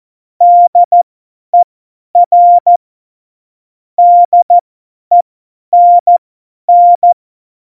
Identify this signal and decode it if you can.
Morse code, heard as a keyed tone, about 14 wpm, spaced as for 10 wpm, 700 Hz. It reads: DER DENN
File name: Morsekurs / Witze / Witz 06. November 2018, 50 F70